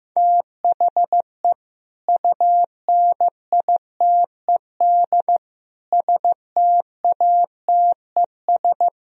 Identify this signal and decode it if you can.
Morse code, heard as a keyed tone, about 15 wpm, 700 Hz. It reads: THE UNITED STATES
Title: Speed-Racing - Sets of 5 Words 15wpm